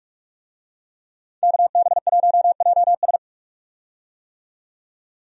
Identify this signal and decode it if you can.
Morse code, heard as a keyed tone, about 46 wpm, 700 Hz. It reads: KB1JS
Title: USA Call Signs - 5 Characters - 45wpm